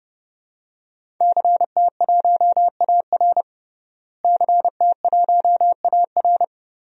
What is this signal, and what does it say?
Morse code, heard as a keyed tone, about 30 wpm, 700 Hz. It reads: CT1AR CT1AR
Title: DXCC-100 Call Signs - 5 Characters - Repeated 3x - 30wpm